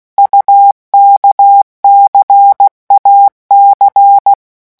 Morse code, heard as a keyed tone, about 16 wpm, 800 Hz. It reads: UKCAC
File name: morse mp3 / revision2-vitesse 15 mots